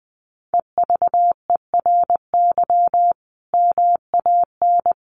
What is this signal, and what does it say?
Morse code, heard as a keyed tone, about 20 wpm, 700 Hz. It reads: EVERY MAN
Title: Sentences from Top 1400 Words 20wpm